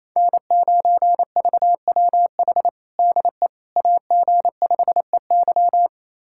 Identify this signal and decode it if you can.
Morse code, heard as a keyed tone, about 28 wpm, 700 Hz. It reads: N9VWH DE AG5EY
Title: USA Callsign DE Callsign - 5 Characters - 28wpm